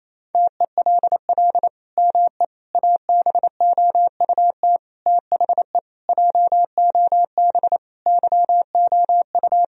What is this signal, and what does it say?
Morse code, heard as a keyed tone, about 27 wpm, 700 Hz. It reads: TELL ME ABOUT THE JOB YOU
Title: Sentences from Top 1000 Words - Review Part 3 - 28wpm